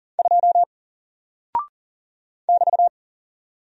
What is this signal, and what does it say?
Morse code, heard as a keyed tone, about 39 wpm, 700 Hz. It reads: 2 <BT>
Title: Single Character (Letter:Number:Punctuation:Prosign) Alt 40wpm